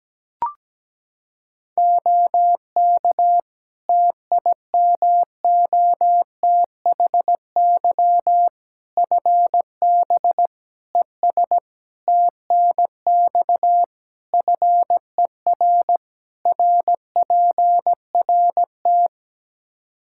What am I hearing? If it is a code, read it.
Morse code, heard as a keyed tone, about 17 wpm, 700 Hz. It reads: OK TIMOTHY FB ES TNX FER RPRT
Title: Practice QSO - Exchange 2 - Info - 17wpm